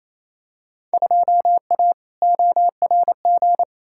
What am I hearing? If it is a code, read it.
Morse code, heard as a keyed tone, about 28 wpm, 700 Hz. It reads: 2A ORG
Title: Field Day Exchanges Encore - 28wpm